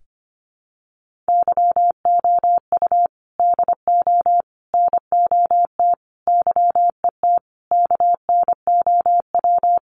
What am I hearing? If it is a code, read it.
Morse code, heard as a keyed tone, about 25 wpm, 700 Hz. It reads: YOU DO NOT YET KNOW
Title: Sentences from Top 1500 Words 25wpm